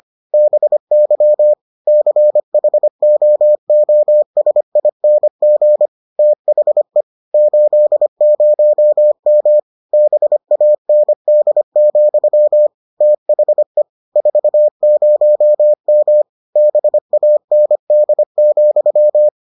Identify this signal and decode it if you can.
Morse code, heard as a keyed tone, about 25 wpm, 600 Hz. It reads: BY CHOOSING THE 80M BAND, THE 40M BAND,